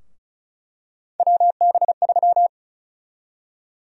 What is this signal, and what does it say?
Morse code, heard as a keyed tone, about 36 wpm, 700 Hz. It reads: WB3
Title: USA Call Sign Prefixes - Repeated 2x - 35wpm